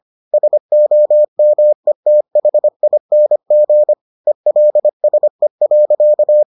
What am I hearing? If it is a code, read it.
Morse code, heard as a keyed tone, about 25 wpm, 600 Hz. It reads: SOMETHING ELSE.